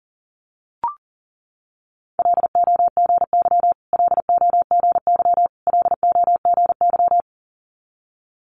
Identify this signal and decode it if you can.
Morse code, heard as a keyed tone, about 37 wpm, 690 Hz. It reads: LOGY LOGY LOGY
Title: Multi-letter Phonemes Repeated 3x 40wpm